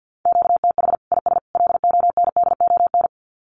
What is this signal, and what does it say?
Morse code, heard as a keyed tone, about 48 wpm, 700 Hz. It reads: QTH IS LONDON
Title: Common QSO Phrases 50wpm